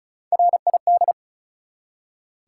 Morse code, heard as a keyed tone, about 36 wpm, 700 Hz. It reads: RID
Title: Three Letter Words - 35wpm